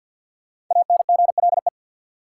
Morse code, heard as a keyed tone, about 53 wpm, 700 Hz. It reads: ANGLE
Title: Sentences from Top 1500 Words - New Word Drill - 50wpm